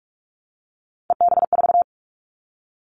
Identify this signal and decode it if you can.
Morse code, heard as a keyed tone, about 44 wpm, 700 Hz. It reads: EB4
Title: Callsign Prefixes for Top 25 DXCC - Repeated 3x - 45wpm